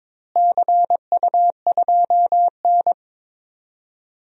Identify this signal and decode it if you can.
Morse code, heard as a keyed tone, about 22 wpm, 700 Hz. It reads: CU2N
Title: DXCC-100 Call Signs - 4 Characters - Repeated 2x - 22wpm